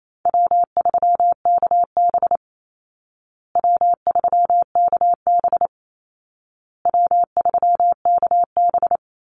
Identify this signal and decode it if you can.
Morse code, heard as a keyed tone, about 28 wpm, 700 Hz. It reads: W3KB W3KB W3KB
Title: Easy Call Signs Repeated 3x - 28wpm